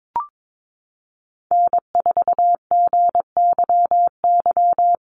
Morse code, heard as a keyed tone, about 22 wpm, 700 Hz. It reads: N4GYY
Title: USA Call Signs Repeated 2x 22wpm